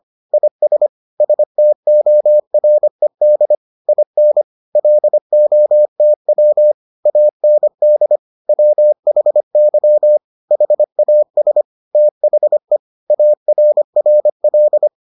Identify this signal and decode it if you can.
Morse code, heard as a keyed tone, about 25 wpm, 600 Hz. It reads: IS STORED IN LOTW AND WHY HAS THE ARRL